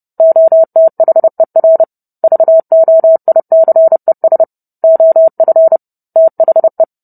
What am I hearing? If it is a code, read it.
Morse code, heard as a keyed tone, about 30 wpm, 650 Hz. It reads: OTHER VOICES OF THE